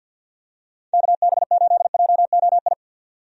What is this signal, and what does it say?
Morse code, heard as a keyed tone, about 52 wpm, 700 Hz. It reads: KB8JOI
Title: USA Call Signs - 6 Characters - 50wpm